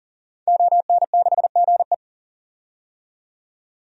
Morse code, heard as a keyed tone, about 40 wpm, 700 Hz. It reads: ON6GE
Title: DXCC-100 Call Signs - 5 Characters - 40wpm